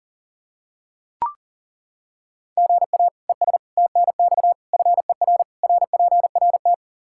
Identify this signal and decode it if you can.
Morse code, heard as a keyed tone, about 40 wpm, 700 Hz. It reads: GA ES TNX FER RPRT
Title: Practice QSO - Exchange 1 - Beginning - 40wpm